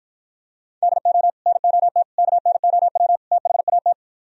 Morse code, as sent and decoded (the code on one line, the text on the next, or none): -.. --- / -. --- - / -.- -. --- .-- / - .... .- -
DO NOT KNOW THAT